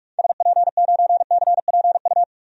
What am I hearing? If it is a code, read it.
Morse code, heard as a keyed tone, about 41 wpm, 700 Hz. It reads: SP9CPU
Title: DXCC-100 Call Signs - 6 Characters - Repeated 3x - 45wpm